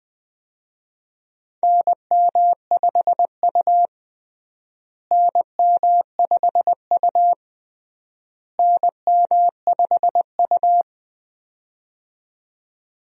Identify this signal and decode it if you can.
Morse code, heard as a keyed tone, about 20 wpm, 700 Hz. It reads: NM5U NM5U NM5U
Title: USA Call Signs - 4 Characters - Repeated 3x - 20wpm